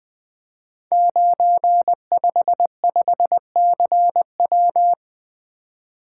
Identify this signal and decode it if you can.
Morse code, heard as a keyed tone, about 20 wpm, 700 Hz. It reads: 955CW
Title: Code Groups - 5 Letter:Numbers - 20wpm